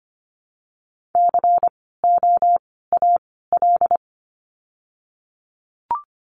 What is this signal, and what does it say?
Morse code, heard as a keyed tone, about 25 wpm, 700 Hz. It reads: C O A L
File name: course / 25:10wpm / 094 Lesson 16f - Letter c - Words 25wpm